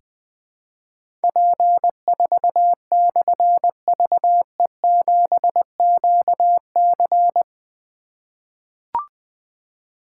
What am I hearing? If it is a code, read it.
Morse code, heard as a keyed tone, about 20 wpm, 700 Hz. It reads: P4/VE7QC
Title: Difficult Call Signs 20wpm